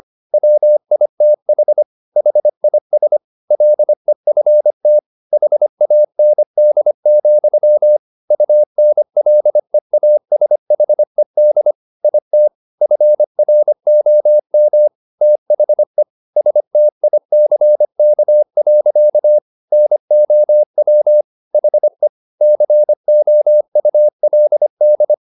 Morse code, heard as a keyed tone, about 25 wpm, 600 Hz. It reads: WITH HIS LEFT HAND, UNLEASHED IT FROM THE STICK. NOW HE COULD
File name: the old man and the sea part 1 25wpm